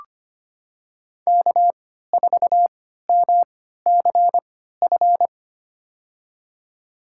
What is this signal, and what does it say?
Morse code, heard as a keyed tone, about 25 wpm, spaced as for 14 wpm, 700 Hz. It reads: K4MCF
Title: USA Call Signs - 5 Characters - 3x Character Spacing - 25wpm